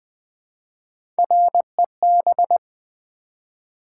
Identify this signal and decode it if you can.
Morse code, heard as a keyed tone, about 20 wpm, 700 Hz. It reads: REB
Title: Build A Word - 1 Letter At A Time - 5-Letter Words - 20wpm